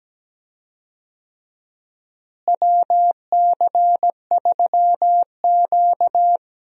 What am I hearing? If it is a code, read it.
Morse code, heard as a keyed tone, about 17 wpm, 700 Hz. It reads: WC3Q
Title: USA Call Signs - 4 Characters - Repeated 2x - 17wpm